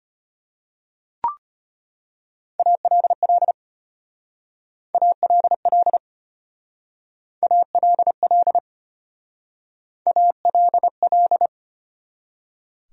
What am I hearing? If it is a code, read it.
Morse code, heard as a keyed tone, about 32 wpm, 700 Hz. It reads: ALL ALL ALL ALL